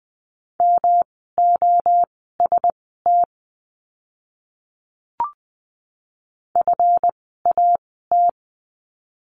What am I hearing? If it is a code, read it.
Morse code, heard as a keyed tone, about 20 wpm, spaced as for 14 wpm, 700 Hz. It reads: MOST FAT
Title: 139 Lesson 23f - Letter f - Words 20wpm